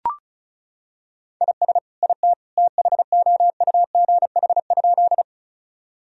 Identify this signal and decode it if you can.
Morse code, heard as a keyed tone, about 34 wpm, 700 Hz. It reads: IS IT THOUGH?
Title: Sentences from Top 1000 Words - Review Part 2 - 35wpm